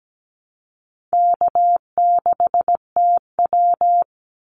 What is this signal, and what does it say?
Morse code, heard as a keyed tone, about 17 wpm, 700 Hz. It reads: K6TW
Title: NA SOTA Activator Callsigns 17wpm